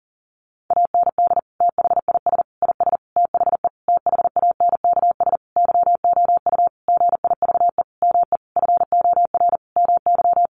Morse code, heard as a keyed tone, about 40 wpm, 700 Hz. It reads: AND THIS IS THE THANKS YOU GIVE ME FOR MY
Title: Sentences from Top 1200 Words 40wpm